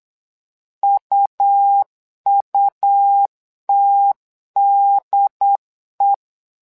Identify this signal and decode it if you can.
Morse code, heard as a keyed tone, about 8 wpm, 800 Hz. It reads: UUTDE
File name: LEZIONI / 3 MWRO / 3 MWRO 8wpm 2606